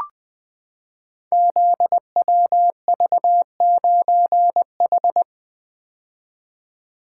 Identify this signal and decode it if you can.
Morse code, heard as a keyed tone, about 20 wpm, 700 Hz. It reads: ZWV9H
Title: Code Groups - 5 Letter:Numbers - 20wpm